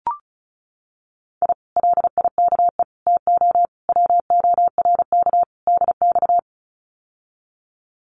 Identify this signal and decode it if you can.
Morse code, heard as a keyed tone, about 34 wpm, 700 Hz. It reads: I LIKE TO WORK DX
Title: Common QSO Phrases Encore 35wpm